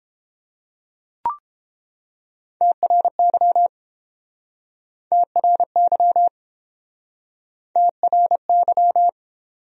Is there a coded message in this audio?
Yes: Morse code from 2.6 to 9.1 s, heard as a keyed tone, about 31 wpm, 700 Hz. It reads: TRY TRY TRY